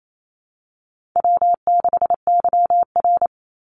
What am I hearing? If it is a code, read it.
Morse code, heard as a keyed tone, about 28 wpm, 700 Hz. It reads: W6YR
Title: Easy Call Signs - 28wpm